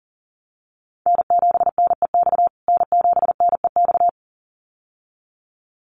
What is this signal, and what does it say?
Morse code, heard as a keyed tone, about 37 wpm, 700 Hz. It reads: N7NEX N7NEX
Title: USA Call Signs Repeated 2x 40wpm